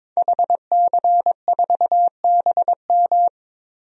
Morse code, heard as a keyed tone, about 22 wpm, 700 Hz. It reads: HC4BM
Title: DXCC-100 Call Signs - 5 Characters - Repeated 3x - 22wpm